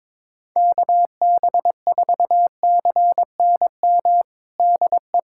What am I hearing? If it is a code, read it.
Morse code, heard as a keyed tone, about 22 wpm, 700 Hz. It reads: KB4CNM DE
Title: USA Callsign DE Callsign 22wpm